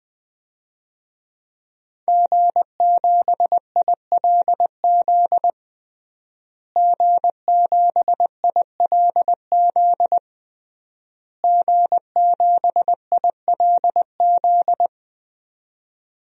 Morse code, heard as a keyed tone, about 20 wpm, 700 Hz. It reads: G7ILZ G7ILZ G7ILZ
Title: DXCC-100 Call Signs - 5 Characters - Repeated 3x - 20wpm